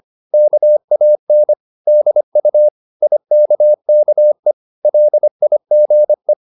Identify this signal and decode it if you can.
Morse code, heard as a keyed tone, about 25 wpm, 600 Hz. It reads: KAN DU IKKE LIGE
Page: Ferie 25 wpm